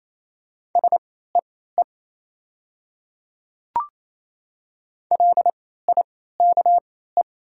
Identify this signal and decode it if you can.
Morse code, heard as a keyed tone, about 28 wpm, spaced as for 16 wpm, 700 Hz. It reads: SEE LIKE
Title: 3x Character Spacing - Top 100 Words - 28wpm